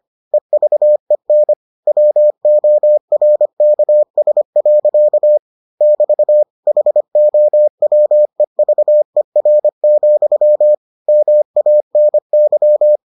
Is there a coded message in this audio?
Yes: Morse code from 0.3 to 13.0 s, heard as a keyed tone, about 25 wpm, 600 Hz. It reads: EVEN WORKS. <BT> HOWEVER, MANY